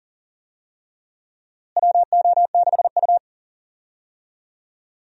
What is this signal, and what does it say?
Morse code, heard as a keyed tone, about 37 wpm, 700 Hz. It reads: WO6U
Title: USA Call Signs - 4 Characters - Repeated 3x - 40wpm